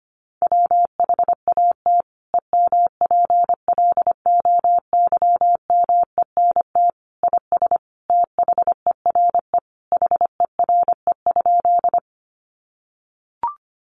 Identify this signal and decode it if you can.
Morse code, heard as a keyed tone, about 25 wpm, 700 Hz. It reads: WHAT EMPLOYMENT IS THERE HERE?
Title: Sentences from Top 1500 Words 25wpm